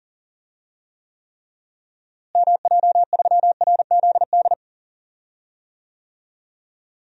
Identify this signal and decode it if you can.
Morse code, heard as a keyed tone, about 39 wpm, 700 Hz. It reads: MJ3RZD
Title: DXCC-100 Call Signs - 6 Characters - Repeated 3x - 40wpm